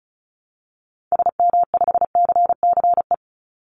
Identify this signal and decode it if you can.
Morse code, heard as a keyed tone, about 34 wpm, 700 Hz. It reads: SM5CCE
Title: Call Signs Repeated 3x - 35wpm